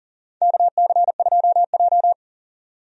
Morse code, heard as a keyed tone, about 40 wpm, 700 Hz. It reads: KC2J
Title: USA Call Signs - 4 Characters - 40wpm